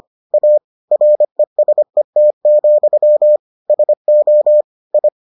Morse code, heard as a keyed tone, about 25 wpm, 600 Hz. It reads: A RESET, SO I